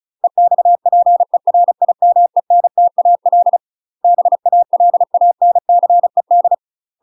Morse code, heard as a keyed tone, about 36 wpm, 700 Hz. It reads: EXPERIMENTAL BALANCED
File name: UK MASTER QSOs TRAINING / 35WPM / UK MASTER QSOs 0009